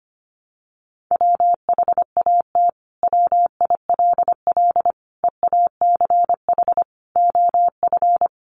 Morse code, heard as a keyed tone, about 25 wpm, 700 Hz. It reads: WHAT WILL EACH OF
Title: Sentences from Top 200 Words 25wpm